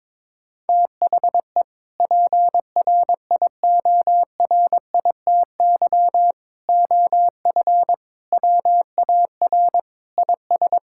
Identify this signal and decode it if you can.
Morse code, heard as a keyed tone, about 22 wpm, 700 Hz. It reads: THE PRIORITY OF WAR IS
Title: Sentences from Top 1600 Words 22wpm